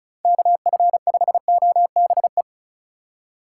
Morse code, heard as a keyed tone, about 36 wpm, 700 Hz. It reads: KF5OBE
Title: USA Call Signs - 6 Characters - 4x Character Spacing - 35wpm